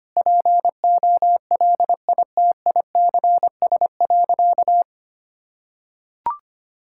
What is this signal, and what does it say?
Morse code, heard as a keyed tone, about 25 wpm, 700 Hz. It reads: POLITICS.
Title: Sentences from Top 1000 Words - Review Part 6 - 25wpm